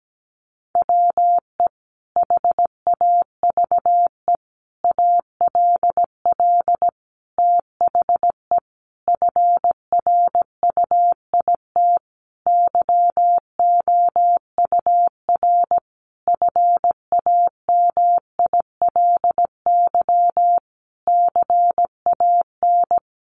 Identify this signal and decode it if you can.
Morse code, heard as a keyed tone, about 17 wpm, 700 Hz. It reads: WE HAVE ALL THE FRUIT YOUR FAMILY CAN
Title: Sentences from Top 1300 Words 17wpm